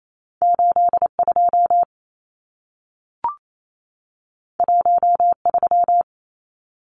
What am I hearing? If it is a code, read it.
Morse code, heard as a keyed tone, about 28 wpm, 700 Hz. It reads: 82 13
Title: Numbers - 2-Digit 28wpm